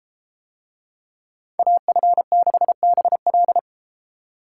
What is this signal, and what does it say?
Morse code, heard as a keyed tone, about 31 wpm, 700 Hz. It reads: AF6BL